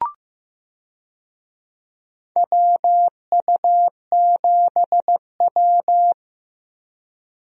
Morse code, heard as a keyed tone, about 15 wpm, 700 Hz. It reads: WU7W
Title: USA Call Signs - 4 Characters - Repeated 3x - 15wpm